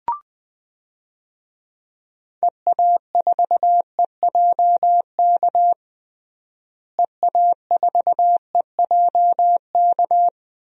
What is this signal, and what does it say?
Morse code, heard as a keyed tone, about 20 wpm, 700 Hz. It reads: EA4EJK EA4EJK
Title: DXCC-100 Call Signs - 6 Characters - Repeated 2x - 20wpm